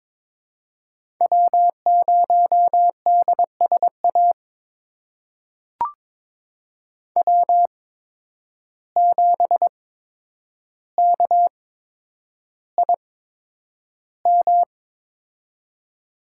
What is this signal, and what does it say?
Morse code, heard as a keyed tone, about 22 wpm, spaced as for 9 wpm, 700 Hz. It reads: W0DSA W7KIM
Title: USA Call Signs - 5 Characters - 8x Character Spacing - 22wpm